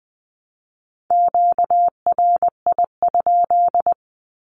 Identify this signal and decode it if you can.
Morse code, heard as a keyed tone, about 20 wpm, 700 Hz. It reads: QRI?
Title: Q-Codes Repeated 3x 20wpm